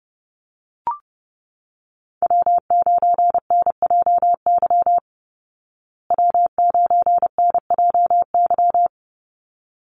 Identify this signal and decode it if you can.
Morse code, heard as a keyed tone, about 30 wpm, 700 Hz. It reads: W9NJY W9NJY
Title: Call Signs Repeated 3x - 30wpm